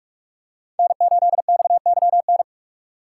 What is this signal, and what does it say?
Morse code, heard as a keyed tone, about 44 wpm, 700 Hz. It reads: N8XYN